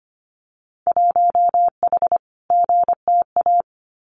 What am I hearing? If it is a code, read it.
Morse code, heard as a keyed tone, about 25 wpm, 700 Hz. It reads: 1H GTA
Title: Winter Field Day Exchanges 25wpm